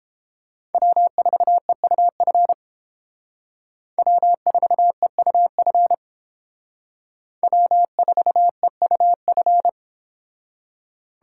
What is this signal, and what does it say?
Morse code, heard as a keyed tone, about 28 wpm, 700 Hz. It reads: W4EUF W4EUF W4EUF